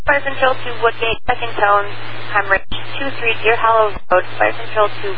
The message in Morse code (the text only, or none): none